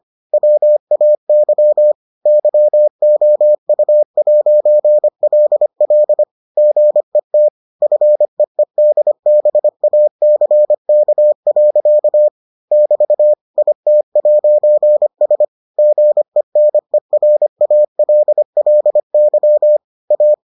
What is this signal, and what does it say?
Morse code, heard as a keyed tone, about 25 wpm, 600 Hz. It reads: WAY YOU'LL GET FEEDBACK. <BT> IT'S GENERALLY A